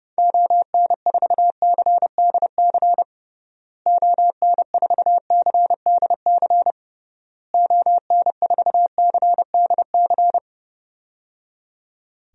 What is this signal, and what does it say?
Morse code, heard as a keyed tone, about 30 wpm, 700 Hz. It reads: ON4CDC ON4CDC ON4CDC
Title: DXCC-100 Call Signs - 6 Characters - Repeated 3x - 30wpm